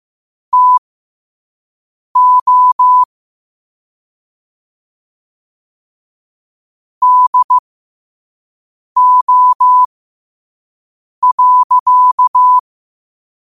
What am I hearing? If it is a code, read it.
Morse code, heard as a keyed tone, about 15 wpm, spaced as for 5 wpm, 1000 Hz. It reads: TO DO.
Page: Quotes for Tue, 12 Aug 2025 in Morse Code at 5 words per minute.